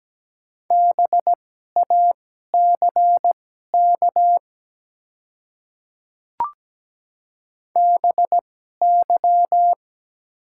Morse code, heard as a keyed tone, about 17 wpm, spaced as for 12 wpm, 700 Hz. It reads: BACK BY
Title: 2x Character Spacing - Top 300 Words - 17wpm